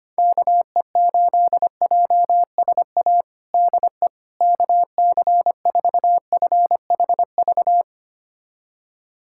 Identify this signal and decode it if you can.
Morse code, heard as a keyed tone, about 25 wpm, 700 Hz. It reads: KE8JSA DE KC4FHV
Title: USA Callsign DE Callsign - 6 Characters - 25wpm